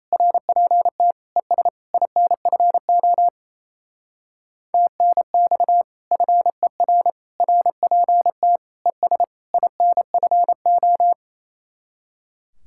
Morse code, heard as a keyed tone, about 34 wpm, 700 Hz. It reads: RPT ES INFO TNX FER RPT ES INFO
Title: Speed-Racing Common QSO Phrases Encore - 28wpm